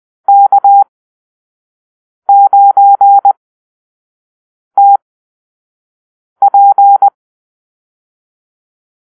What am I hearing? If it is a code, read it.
Morse code, heard as a keyed tone, about 20 wpm, spaced as for 5 wpm, 800 Hz. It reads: K9TP
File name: arrl / Call-Sign Drill-1